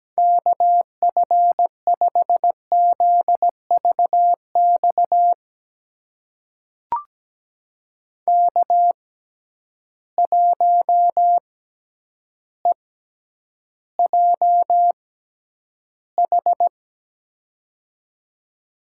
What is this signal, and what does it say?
Morse code, heard as a keyed tone, about 17 wpm, spaced as for 9 wpm, 700 Hz. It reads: KF5ZVX K1EJH
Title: USA Call Signs - 6x Character Spacing - 17wpm